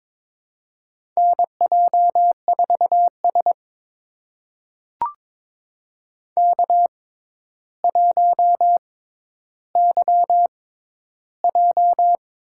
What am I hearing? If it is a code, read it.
Morse code, heard as a keyed tone, about 22 wpm, spaced as for 11 wpm, 700 Hz. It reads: NJ4S K1YJ
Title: USA Call Signs - 4 Characters - 6x Character Spacing - 22wpm